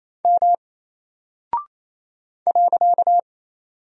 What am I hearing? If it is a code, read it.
Morse code, heard as a keyed tone, about 27 wpm, 700 Hz. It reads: M .
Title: Single Character (Letter:Number:Punctuation:Prosign) 28wpm